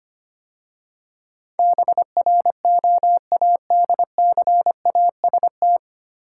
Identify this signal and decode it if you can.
Morse code, heard as a keyed tone, about 25 wpm, 700 Hz. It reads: BROADCAST
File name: practice / 25wpm / 9 Letter Words - First 3 Letters Given - 25wpm